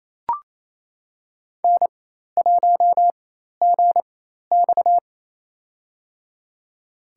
Morse code, heard as a keyed tone, about 28 wpm, spaced as for 13 wpm, 700 Hz. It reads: N1GX
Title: USA Call Signs - 4 Characters - 4x Character Spacing - 28wpm